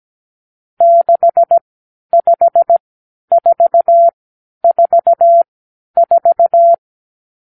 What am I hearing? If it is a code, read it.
Morse code, heard as a keyed tone, about 17 wpm, spaced as for 11 wpm, 680 Hz. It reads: 65444
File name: k7qo / Lesson 50 (1-6)